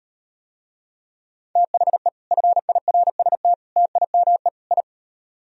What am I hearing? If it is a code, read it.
Morse code, heard as a keyed tone, about 38 wpm, 700 Hz. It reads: THE FIRST TIME I